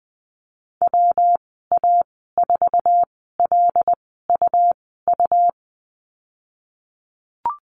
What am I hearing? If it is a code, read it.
Morse code, heard as a keyed tone, about 20 wpm, spaced as for 15 wpm, 700 Hz. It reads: WA4LUU
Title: 128 Lesson 21g - Number 6 - Call Signs 20wpm